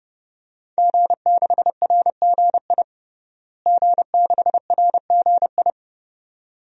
Morse code, heard as a keyed tone, about 30 wpm, 700 Hz. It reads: G6RGI G6RGI
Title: DXCC-100 Call Signs - 5 Characters - Repeated 2x - 30wpm